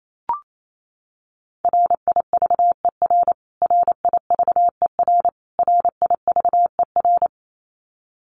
Morse code, heard as a keyed tone, about 28 wpm, 700 Hz. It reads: RIVER RIVER RIVER
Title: Top 500 Words Repeated 28wpm